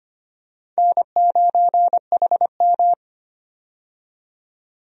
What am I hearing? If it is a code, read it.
Morse code, heard as a keyed tone, about 25 wpm, 700 Hz. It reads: N9HM
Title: USA Call Signs - 4 Characters - 25wpm